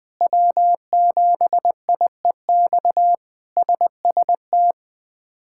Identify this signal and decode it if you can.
Morse code, heard as a keyed tone, about 21 wpm, 700 Hz. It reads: W7IEX SST
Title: K1USN SST QSO Practice - 20wpm